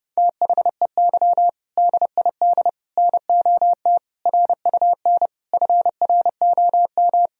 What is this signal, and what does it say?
Morse code, heard as a keyed tone, about 30 wpm, 700 Hz. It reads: THEY DID NOT RUN FROM
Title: Sentences from Top 1000 Words - Review Part 1 - 30wpm